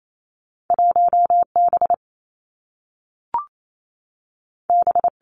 Morse code, heard as a keyed tone, about 28 wpm, 700 Hz. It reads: 1 B B
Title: Two Challenging Letter-Number Combinations 28wpm